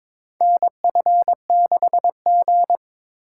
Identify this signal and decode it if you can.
Morse code, heard as a keyed tone, about 22 wpm, 700 Hz. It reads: NF6G
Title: USA Call Signs - 4 Characters - 4x Character Spacing - 22wpm